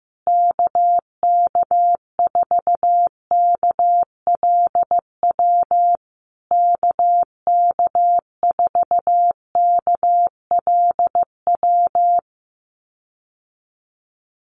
Morse code, heard as a keyed tone, about 15 wpm, 700 Hz. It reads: KK4KLW KK4KLW
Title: USA Call Signs Repeated 3x 15wpm